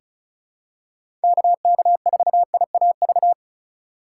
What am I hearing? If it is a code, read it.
Morse code, heard as a keyed tone, about 36 wpm, 700 Hz. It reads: KK4IAV
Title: USA Call Signs - 6 Characters - 4x Character Spacing - 35wpm